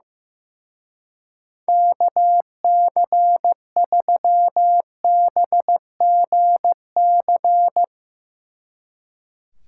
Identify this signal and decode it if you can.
Morse code, heard as a keyed tone, about 16 wpm, 700 Hz. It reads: KC3BGC
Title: Speed Racing - USA Call Signs 15wpm